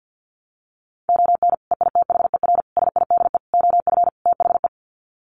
Morse code, heard as a keyed tone, about 50 wpm, 700 Hz. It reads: ON EITHER SIDE OF THE
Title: Sets of 5 Words 50wpm